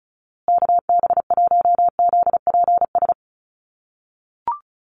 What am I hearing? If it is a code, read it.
Morse code, heard as a keyed tone, about 34 wpm, 700 Hz. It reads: KB1ZPS
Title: USA Call Signs Repeated 2x 35wpm